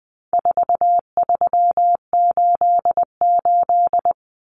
Morse code, heard as a keyed tone, about 20 wpm, 700 Hz. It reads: 4388
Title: Common Cut Numbers - 4-Digit 20wpm